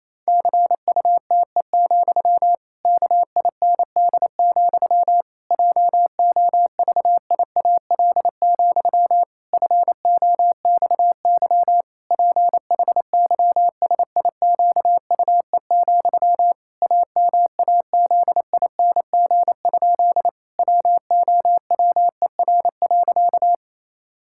Morse code, heard as a keyed tone, about 28 wpm, 700 Hz. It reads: CUTE, KIND, JOVIAL, FOXY PHYSIQUE, AMAZING? WOWER.
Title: Pangram Challenge v2 - 28wpm